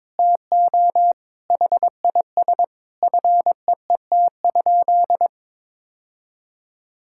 Morse code, heard as a keyed tone, about 22 wpm, 700 Hz. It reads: TO HIS FEET?
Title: Sentences from Top 1000 Words - Review Part 6 - 22wpm